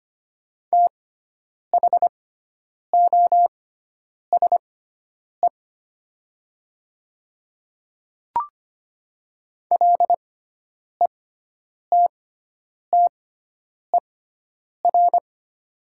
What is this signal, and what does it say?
Morse code, heard as a keyed tone, about 25 wpm, spaced as for 9 wpm, 700 Hz. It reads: THOSE LETTER
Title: 6x Character Spacing - Top 300 Words - 25wpm